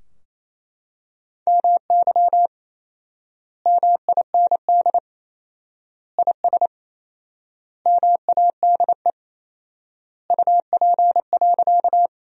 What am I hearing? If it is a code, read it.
Morse code, heard as a keyed tone, about 28 wpm, 700 Hz. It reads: MY MIND IS MADE UP.
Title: 4x Word-Spacing - Sentences from Top 500 Words - 28wpm